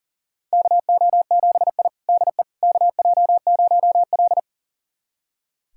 Morse code, heard as a keyed tone, about 40 wpm, 700 Hz. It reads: KO7I DE KJ0L
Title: USA Callsign DE Callsign - 4 Characters - 40wpm